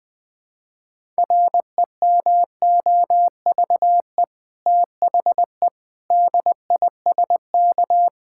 Morse code, heard as a keyed tone, about 20 wpm, 700 Hz. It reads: REMOVE THE DISK
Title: Sentences from Top 1000 Words - Review Part 6 - 20wpm